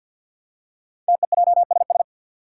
Morse code, heard as a keyed tone, about 48 wpm, 700 Hz. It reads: TEJSS
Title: Code Groups - 5 Letters - 50wpm